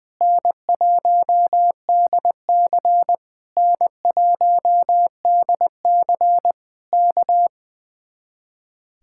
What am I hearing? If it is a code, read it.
Morse code, heard as a keyed tone, about 20 wpm, 700 Hz. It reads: N1DC N1DC K
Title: Speed-Racing Common QSO Phrases - 20wpm